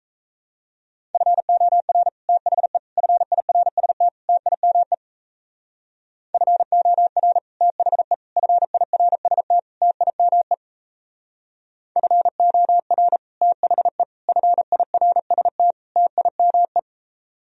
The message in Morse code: FOR THE FIRST TIME FOR THE FIRST TIME FOR THE FIRST TIME

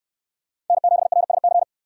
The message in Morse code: N7RSC

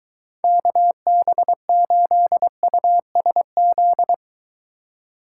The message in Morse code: KB8USZ

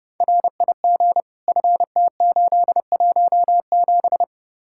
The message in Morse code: RIG FT817

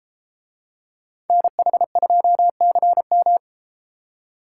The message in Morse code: NH2CM

